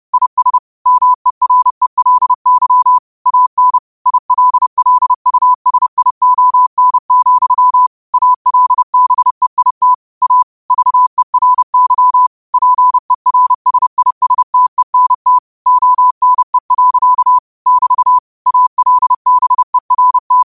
Morse code, heard as a keyed tone, about 30 wpm, 1000 Hz. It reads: IS MERELY AN ILLUSION, ALBEIT A VERY PERSISTENT ONE. <BT> ALBERT
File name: 30wpm